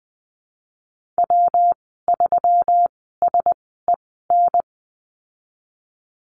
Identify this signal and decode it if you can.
Morse code, heard as a keyed tone, about 20 wpm, spaced as for 15 wpm, 700 Hz. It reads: W3SEN
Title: 154 Lesson 25g - Letter p - Call Signs 20wpm